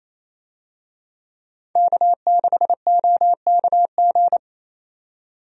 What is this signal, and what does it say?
Morse code, heard as a keyed tone, about 28 wpm, 700 Hz. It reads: K6OKG
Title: USA Call Signs - 5 Characters - 28wpm